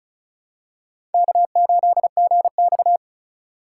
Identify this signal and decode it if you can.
Morse code, heard as a keyed tone, about 36 wpm, 700 Hz. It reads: K8GX
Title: USA Call Signs - 4 Characters - 3x Character Spacing - 35wpm